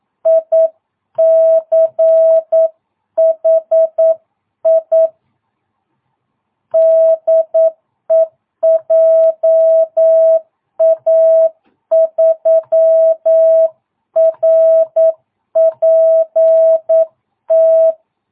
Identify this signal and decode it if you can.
Morse code, heard as a keyed tone, about 9 wpm, 660 Hz. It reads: ICHI DE JA3RPT
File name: jh8cbh / 2022cw